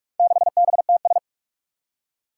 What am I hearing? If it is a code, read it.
Morse code, heard as a keyed tone, about 41 wpm, 700 Hz. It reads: 6BTS
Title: Four Challenging Letter-Number Combinations 45wpm